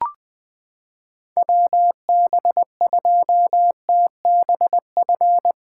Morse code, heard as a keyed tone, about 20 wpm, 700 Hz. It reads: WB2TBF
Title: SOTA QSO Practice 20wpm